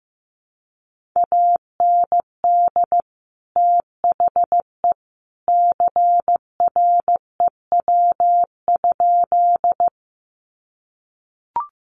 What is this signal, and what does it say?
Morse code, heard as a keyed tone, about 15 wpm, 700 Hz. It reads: AND THE CREW?
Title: Sentences from Top 1500 Words 15wpm